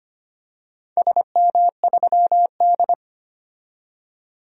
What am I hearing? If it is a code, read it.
Morse code, heard as a keyed tone, about 25 wpm, 700 Hz. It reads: SM3D
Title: DXCC-100 Call Signs - 4 Characters - 25wpm